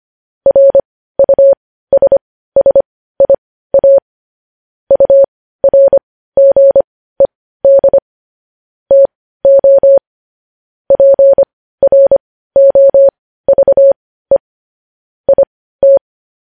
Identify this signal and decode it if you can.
Morse code, heard as a keyed tone, about 25 wpm, spaced as for 15 wpm, 550 Hz. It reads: RUSSIA URGED TO PROVE IT
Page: news-headlines-in-morse-code-at-15-wpm